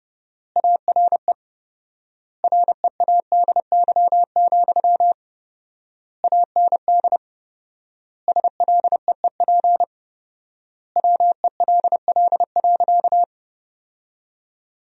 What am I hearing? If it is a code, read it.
Morse code, heard as a keyed tone, about 32 wpm, 700 Hz. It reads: ARE READY, AND SLEEP WELL.
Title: 4x Word-Spacing - Sentences from Top 500 Words - 30wpm